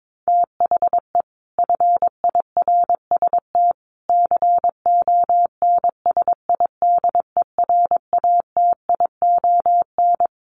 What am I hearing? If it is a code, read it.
Morse code, heard as a keyed tone, about 22 wpm, 700 Hz. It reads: THE FIRST CONSIDERATION
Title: Sentences from Top 1500 Words 22wpm